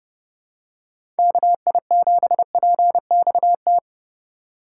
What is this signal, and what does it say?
Morse code, heard as a keyed tone, about 32 wpm, 700 Hz. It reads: KI7PXT
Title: USA Call Signs - 2x Character Spacing - 30wpm